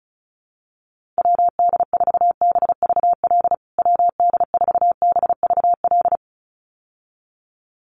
Morse code, heard as a keyed tone, about 34 wpm, 700 Hz. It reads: WD4BVL WD4BVL
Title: USA Call Signs Repeated 2x 35wpm